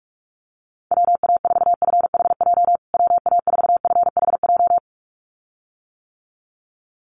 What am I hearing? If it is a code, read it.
Morse code, heard as a keyed tone, about 44 wpm, 700 Hz. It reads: WA4FHJ WA4FHJ
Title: USA Call Signs Repeated 2x 45wpm